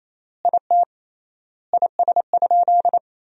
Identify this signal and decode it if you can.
Morse code, heard as a keyed tone, about 27 wpm, 700 Hz. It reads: IT IS?
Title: 3x Word-Spacing - Sentences from Top 100 Words - 28wpm